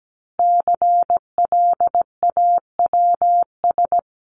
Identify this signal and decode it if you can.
Morse code, heard as a keyed tone, about 17 wpm, 700 Hz. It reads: CLAWS
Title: A Sound Of Thunder 17wpm